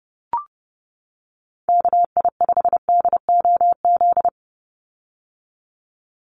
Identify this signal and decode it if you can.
Morse code, heard as a keyed tone, about 30 wpm, 700 Hz. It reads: KI5DOZ
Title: USA Call Signs 30wpm